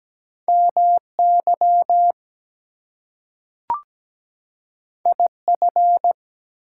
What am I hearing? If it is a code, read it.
Morse code, heard as a keyed tone, about 17 wpm, 700 Hz. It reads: MY IF
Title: Two Letter Words - 17wpm